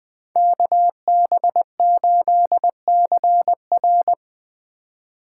Morse code, heard as a keyed tone, about 20 wpm, 700 Hz. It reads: KB8CR